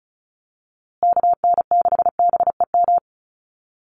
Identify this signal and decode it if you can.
Morse code, heard as a keyed tone, about 34 wpm, 700 Hz. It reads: KN6BEM
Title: USA Call Signs Repeated 3x 35wpm